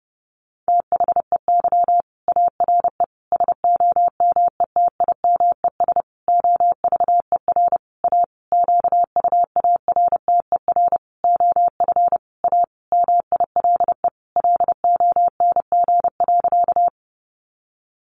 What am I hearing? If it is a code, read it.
Morse code, heard as a keyed tone, about 30 wpm, 700 Hz. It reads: THEY ARE SOMETIMES OVER A QUARTER OF A MILE LONG.
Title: Sentences from Top 1300 Words 30wpm